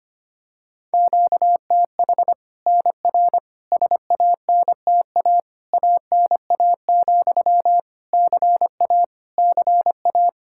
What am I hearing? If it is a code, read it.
Morse code, heard as a keyed tone, about 25 wpm, 700 Hz. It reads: QTH NR SANTA ANA, CA CA
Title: Practice QSO - Exchange 1 - Beginning - 25wpm